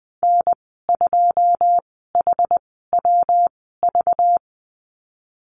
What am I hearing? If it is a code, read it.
Morse code, heard as a keyed tone, about 20 wpm, spaced as for 15 wpm, 700 Hz. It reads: N2HWV
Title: 207 Lesson 33g - Letter j - Call Signs 20wpm